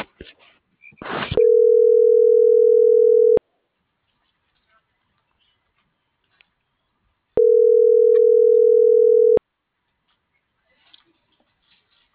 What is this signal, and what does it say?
Ringback tone (North America), 440 + 480 Hz.